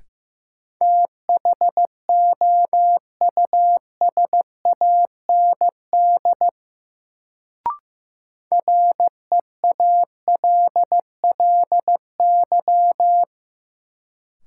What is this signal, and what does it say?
Morse code, heard as a keyed tone, about 15 wpm, 700 Hz. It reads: THOUSAND REALLY
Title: Sentences from Top 600 Words - New Word Drill - 15wpm